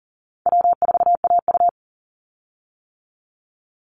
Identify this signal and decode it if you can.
Morse code, heard as a keyed tone, about 40 wpm, 700 Hz. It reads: W4AU
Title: Call Signs - 40wpm